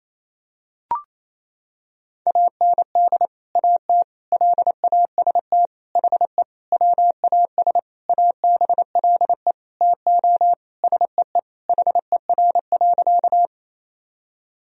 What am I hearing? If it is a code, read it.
Morse code, heard as a keyed tone, about 28 wpm, 700 Hz. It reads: AND AT LAST HE WAS ABLE TO SEE HER.
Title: Sentences from Top 1000 Words - Review Part 2 - 28wpm